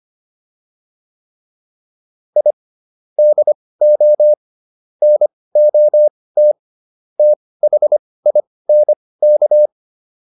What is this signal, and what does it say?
Morse code, heard as a keyed tone, about 25 wpm, spaced as for 18 wpm, 600 Hz. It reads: I DO NOT THINK